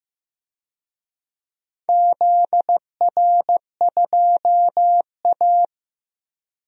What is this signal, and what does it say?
Morse code, heard as a keyed tone, about 15 wpm, 700 Hz. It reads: ZR2A
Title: DXCC-100 Call Signs - 4 Characters - Repeated 3x - 15wpm